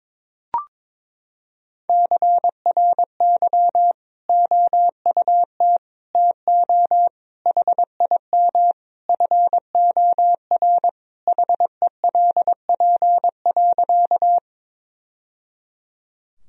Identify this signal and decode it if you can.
Morse code, heard as a keyed tone, about 22 wpm, 700 Hz. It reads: CRY OUT TO HIM FOR HELP.
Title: Sentences from Top 1000 Words - Review Part 2 - 22wpm